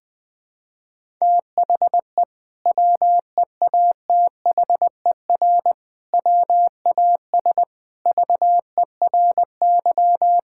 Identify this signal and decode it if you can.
Morse code, heard as a keyed tone, about 20 wpm, 700 Hz. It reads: THE WEATHER WAS VERY
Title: Sentences from Top 1000 Words - Review Part 5 - 20wpm